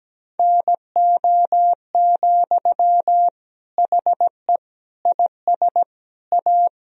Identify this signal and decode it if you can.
Morse code, heard as a keyed tone, about 17 wpm, 700 Hz. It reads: NO, HE IS A
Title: Sentences from Top 1000 Words - Review Part 2 - 17wpm